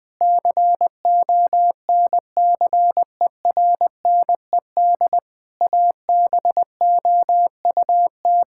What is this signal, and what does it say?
Morse code, heard as a keyed tone, about 20 wpm, 700 Hz. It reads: CONCERNED ABOUT
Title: Sentences from Top 1000 Words - Review Part 6 - 20wpm